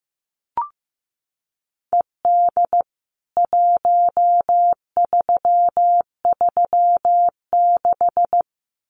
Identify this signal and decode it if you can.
Morse code, heard as a keyed tone, about 14 wpm, 700 Hz. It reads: ED 1336
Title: CWT Exchange 15wpm